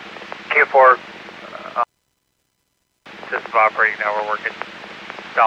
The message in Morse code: none

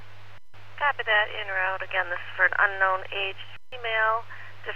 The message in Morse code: none

